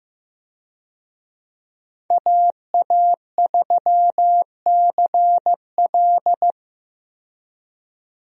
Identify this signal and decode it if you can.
Morse code, heard as a keyed tone, about 15 wpm, 700 Hz. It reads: AA3CL
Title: USA Call Signs - 5 Characters - 15wpm